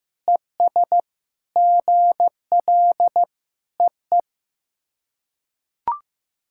Morse code, heard as a keyed tone, about 16 wpm, 700 Hz. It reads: ES GL EE
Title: SOTA QSO Practice 15wpm